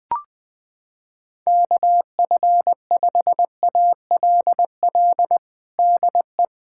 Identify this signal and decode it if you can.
Morse code, heard as a keyed tone, about 21 wpm, 700 Hz. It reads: KF5ALL DE
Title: USA Callsign DE Callsign 20wpm